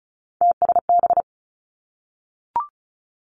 Morse code, heard as a keyed tone, about 36 wpm, 700 Hz. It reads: TSB
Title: Three Challenging Letter-Number Combinations 35wpm